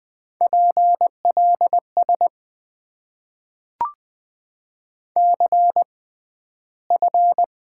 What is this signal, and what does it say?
Morse code, heard as a keyed tone, about 20 wpm, spaced as for 12 wpm, 700 Hz. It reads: PLS CF
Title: Top 90 QSO Elements - 6x Character Spacing - 20wpm